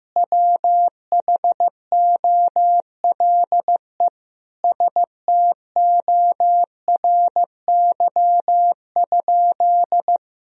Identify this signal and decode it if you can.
Morse code, heard as a keyed tone, about 15 wpm, 700 Hz. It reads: WHOLE STORY?
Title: Sentences from Top 400 Words and Abbreviations 15wpm